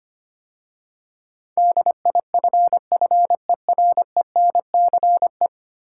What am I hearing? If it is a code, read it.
Morse code, heard as a keyed tone, about 25 wpm, 700 Hz. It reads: DIFFERENCE
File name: 10 Letter Words - First 3 Letters Given - 25wpm